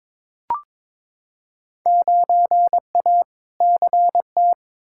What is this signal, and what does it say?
Morse code, heard as a keyed tone, about 22 wpm, 700 Hz. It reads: 9A CT
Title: Field Day Exchanges Encore - 22wpm